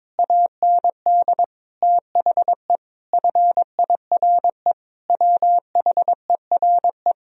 Morse code, heard as a keyed tone, about 21 wpm, 700 Hz. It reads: AND THE FIRE WHERE
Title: Sentences from Top 1000 Words - Review Part 2 - 22wpm